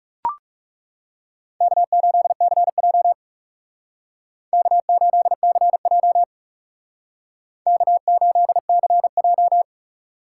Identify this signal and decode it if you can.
Morse code, heard as a keyed tone, about 36 wpm, 700 Hz. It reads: K8CJ K8CJ K8CJ
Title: Speed Racing - USA Call Signs 30wpm